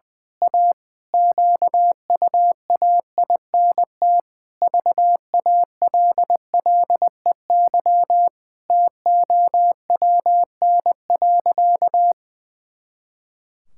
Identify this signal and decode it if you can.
Morse code, heard as a keyed tone, about 19 wpm, 700 Hz. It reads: A QUAINT VALLEY TOWN.
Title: Pangram Challenge v2 - 20wpm